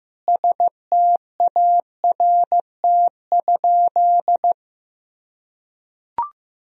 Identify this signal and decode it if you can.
Morse code, heard as a keyed tone, about 16 wpm, 700 Hz. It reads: START?
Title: Sentences from Top 1000 Words - Review Part 1 - 15wpm